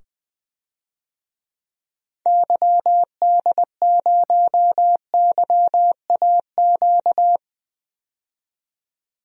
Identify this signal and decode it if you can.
Morse code, heard as a keyed tone, about 20 wpm, 700 Hz. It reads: YD0YAQ
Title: DXCC-100 Call Signs - 6 Characters - 20wpm